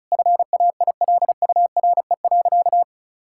Morse code, heard as a keyed tone, about 36 wpm, 700 Hz. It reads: FAILURE.
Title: Sentences from Top 1000 Words - Review Part 6 - 35wpm